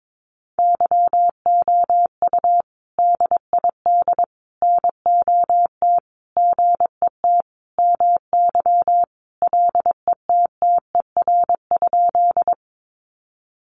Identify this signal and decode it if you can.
Morse code, heard as a keyed tone, about 22 wpm, 700 Hz. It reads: YOU DID NOT GET MY LETTER?
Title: Sentences from Top 600 Words 22wpm